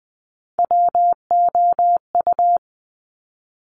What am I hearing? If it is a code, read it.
Morse code, heard as a keyed tone, about 20 wpm, 700 Hz. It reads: WOU
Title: Top Three Letter Combinations in Words 20wpm